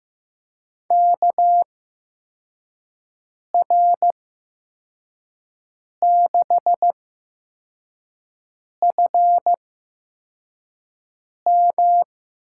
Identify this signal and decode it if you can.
Morse code, heard as a keyed tone, about 15 wpm, spaced as for 4 wpm, 700 Hz. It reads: KR6FM
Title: USA Call Signs - 8x Character Spacing - 15wpm